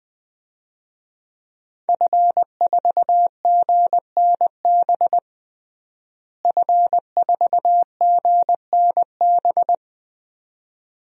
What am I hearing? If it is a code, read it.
Morse code, heard as a keyed tone, about 21 wpm, 700 Hz. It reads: F4GNB F4GNB
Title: DXCC-100 Call Signs - 5 Characters - Repeated 2x - 20wpm